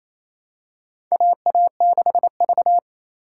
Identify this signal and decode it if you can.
Morse code, heard as a keyed tone, about 28 wpm, 700 Hz. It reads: AA6V
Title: USA Call Signs - 4 Characters - 2x Character Spacing - 28wpm